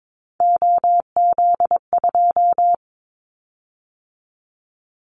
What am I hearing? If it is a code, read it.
Morse code, heard as a keyed tone, about 22 wpm, 700 Hz. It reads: OZ2
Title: Callsign Prefixes for Top 100 DXCC - 22wpm